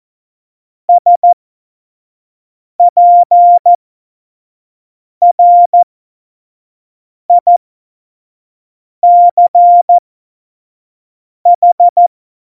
Morse code, heard as a keyed tone, about 14 wpm, spaced as for 5 wpm, 700 Hz. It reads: SPRICH
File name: Morsekurs / Witze / Witz 05. September 17, Tempo 25